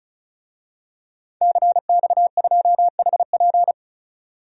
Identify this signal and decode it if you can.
Morse code, heard as a keyed tone, about 36 wpm, 700 Hz. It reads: CX2HP
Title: DXCC-100 Call Signs - 5 Characters - 35wpm